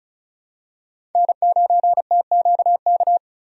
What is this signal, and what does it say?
Morse code, heard as a keyed tone, about 36 wpm, 700 Hz. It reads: N9TQK
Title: USA Call Signs - 5 Characters - Repeated 3x - 35wpm